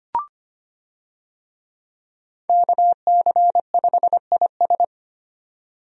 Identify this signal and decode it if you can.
Morse code, heard as a keyed tone, about 25 wpm, 700 Hz. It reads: KC5IS
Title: USA Call Signs - 5 Characters - Repeated 3x - 25wpm